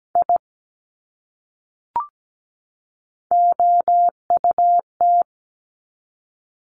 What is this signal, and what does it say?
Morse code, heard as a keyed tone, about 17 wpm, 700 Hz. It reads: I OUT
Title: Top 200 Words 17wpm